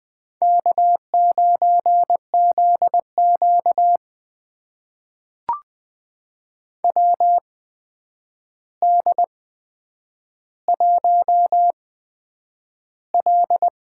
Morse code, heard as a keyed tone, about 21 wpm, spaced as for 9 wpm, 700 Hz. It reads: K9ZQ WD1L
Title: USA Call Signs - 4 Characters - 8x Character Spacing - 20wpm